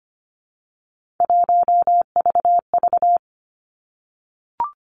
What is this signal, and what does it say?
Morse code, heard as a keyed tone, about 25 wpm, 700 Hz. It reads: 1VV
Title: Three Challenging Letter-Number Combinations 25wpm